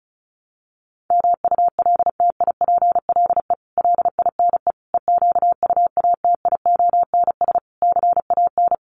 Morse code, heard as a keyed tone, about 34 wpm, 700 Hz. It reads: MULTIPLE LINE EQUATIONS CAN
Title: Sets of 5 Words 35wpm